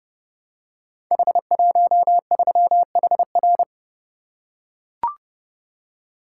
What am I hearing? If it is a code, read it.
Morse code, heard as a keyed tone, about 28 wpm, 700 Hz. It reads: H13HR
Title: Code Groups - 5 Letter:Numbers - 30wpm